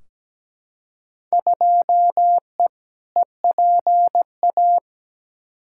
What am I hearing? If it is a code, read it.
Morse code, heard as a keyed tone, about 17 wpm, 700 Hz. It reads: 2E EPA
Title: Field Day Exchanges Encore - 17wpm